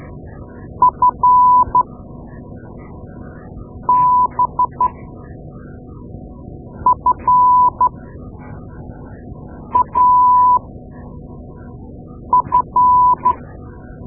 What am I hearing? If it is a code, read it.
Morse code, heard as a keyed tone, about 13 wpm, spaced as for 4 wpm, 1000 Hz. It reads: FBFAF
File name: Morse Lessons / Morse Lessons / Tape-2A